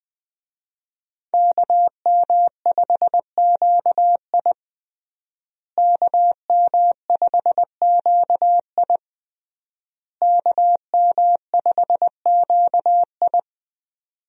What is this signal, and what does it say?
Morse code, heard as a keyed tone, about 20 wpm, 700 Hz. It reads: KM5QI KM5QI KM5QI
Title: USA Call Signs - 5 Characters - Repeated 3x - 20wpm